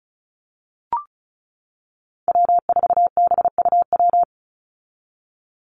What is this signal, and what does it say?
Morse code, heard as a keyed tone, about 34 wpm, 700 Hz. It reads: W4BUW
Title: Call Signs - 35wpm